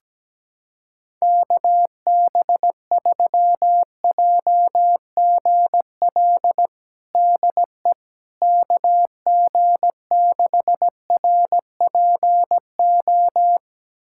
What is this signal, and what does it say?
Morse code, heard as a keyed tone, about 17 wpm, 700 Hz. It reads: KB3JGL DE KG6RPO
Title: USA Callsign DE Callsign 17wpm